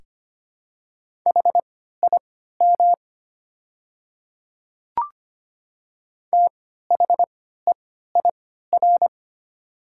Morse code, heard as a keyed tone, about 25 wpm, spaced as for 14 wpm, 700 Hz. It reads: HIM THEIR
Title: 3x Character Spacing - Top 100 Words - 25wpm